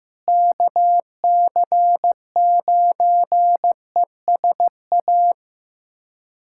Morse code, heard as a keyed tone, about 15 wpm, 700 Hz. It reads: KC9ESA
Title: USA Callsign DE Callsign - 6 Characters - 15wpm